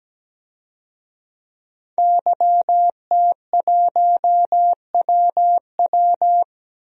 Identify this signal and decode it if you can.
Morse code, heard as a keyed tone, about 17 wpm, 700 Hz. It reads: YT1WW
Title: DXCC-100 Call Signs - 5 Characters - 17wpm